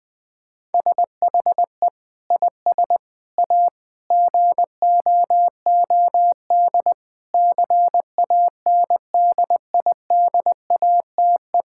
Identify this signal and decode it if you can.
Morse code, heard as a keyed tone, about 21 wpm, 700 Hz. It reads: SHE IS A GOOD CANDIDATE
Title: Sentences from Top 1600 Words 20wpm